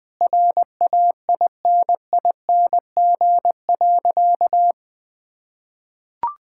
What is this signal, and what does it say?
Morse code, heard as a keyed tone, about 20 wpm, 700 Hz. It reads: RAINING.
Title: Sentences with 4-Word Phrases - 20wpm